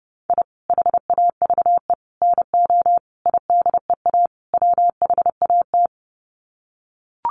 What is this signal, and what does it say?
Morse code, heard as a keyed tone, about 28 wpm, 700 Hz. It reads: I HAVE NO IDEA WHAT
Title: Sets of 5 Words Encore 30wpm